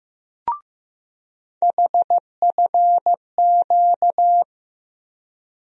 Morse code, heard as a keyed tone, about 15 wpm, 700 Hz. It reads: HFQ
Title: USA Call Sign Postfixes - 3 Letters 15wpm